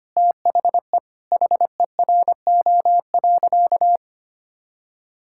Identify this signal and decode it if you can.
Morse code, heard as a keyed tone, about 25 wpm, 700 Hz. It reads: THE HERO.
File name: practice / 25wpm / Sentences with 4-Word Phrases - 25wpm